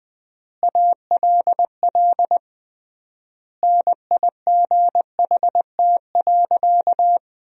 Morse code, heard as a keyed tone, about 20 wpm, 700 Hz. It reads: ALL NIGHT.
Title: 3x Word-Spacing - Sentences from Top 300 Words - 20wpm